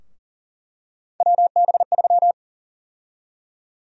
Morse code, heard as a keyed tone, about 40 wpm, 700 Hz. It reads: WB3
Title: USA Call Sign Prefixes - 40wpm